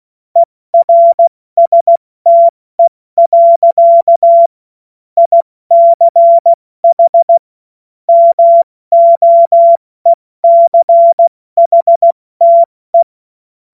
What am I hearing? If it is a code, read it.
Morse code, heard as a keyed tone, about 16 wpm, 680 Hz. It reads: ERSTE. ICH MOECHTE
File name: Morsekurs / Witze / Witz 04. Januar 2022, 70 F80